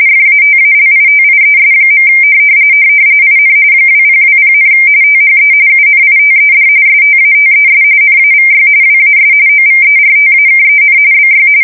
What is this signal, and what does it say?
RTTY, about 45.45 baud. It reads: FATALITIES. IN 2020, THE UNITED STATES NAVY WAS OVERTAKEN BY
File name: fsk170 / usnavy5